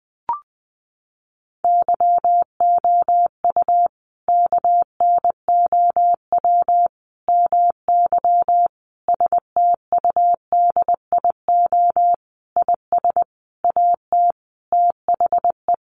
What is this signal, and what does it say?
Morse code, heard as a keyed tone, about 20 wpm, 700 Hz. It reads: YOU KNOW MY STUDIO IS AT THE
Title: Sentences from Top 1200 Words 20wpm